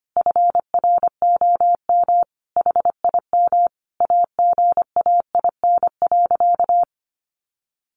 Morse code, heard as a keyed tone, about 25 wpm, 700 Hz. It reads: FROM HIM AGAIN.
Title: Sentences from Top 1300 Words 25wpm